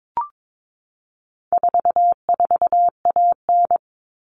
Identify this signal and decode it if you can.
Morse code, heard as a keyed tone, about 22 wpm, 700 Hz. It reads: 44AN
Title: All Cut Numbers - 4-Digit 22wpm